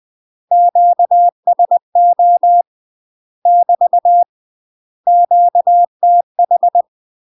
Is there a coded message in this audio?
Yes: Morse code from 0.5 to 6.8 s, heard as a keyed tone, about 20 wpm, 700 Hz. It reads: QSO <BT> QTH